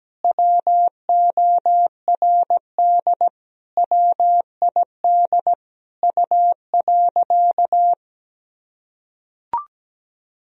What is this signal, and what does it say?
Morse code, heard as a keyed tone, about 17 wpm, 700 Hz. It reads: WORD WID U.
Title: Sentences from Top 300 Words and Abbreviations 17wpm